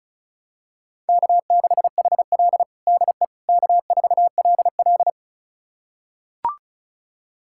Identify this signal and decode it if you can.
Morse code, heard as a keyed tone, about 34 wpm, 700 Hz. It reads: K6HL DE K4LL
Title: USA Callsign DE Callsign - 4 Characters - 35wpm